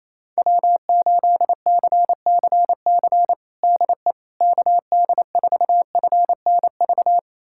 Morse code, heard as a keyed tone, about 28 wpm, 700 Hz. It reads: W8CCC DE KD4FNV
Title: USA Callsign DE Callsign 28wpm